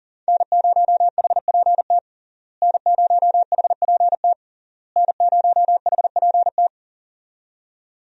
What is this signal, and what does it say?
Morse code, heard as a keyed tone, about 40 wpm, 700 Hz. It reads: N0HPT N0HPT N0HPT
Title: USA Call Signs - 5 Characters - Repeated 3x - 40wpm